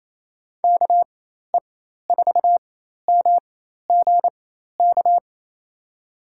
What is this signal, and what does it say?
Morse code, heard as a keyed tone, about 28 wpm, spaced as for 13 wpm, 700 Hz. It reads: KE4MGK
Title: USA Call Signs - 6 Characters - 4x Character Spacing - 28wpm